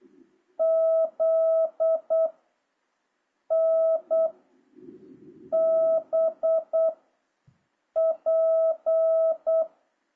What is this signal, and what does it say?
Morse code, heard as a keyed tone, about 8 wpm, 650 Hz. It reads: Z N B P
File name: jh8cbh / 2026cw